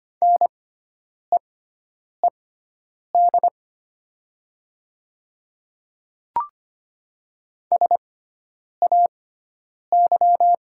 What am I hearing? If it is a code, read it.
Morse code, heard as a keyed tone, about 25 wpm, spaced as for 9 wpm, 700 Hz. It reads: NEED SAY
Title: 6x Character Spacing - Top 300 Words - 25wpm